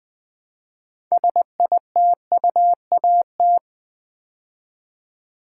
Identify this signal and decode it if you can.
Morse code, heard as a keyed tone, about 20 wpm, 700 Hz. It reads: SITUAT
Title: Build A Word - 2 Letters At A Time - 8-Letter Words - 20wpm